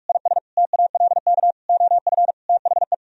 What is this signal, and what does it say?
Morse code, heard as a keyed tone, about 49 wpm, 700 Hz. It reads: IS TALK OF THE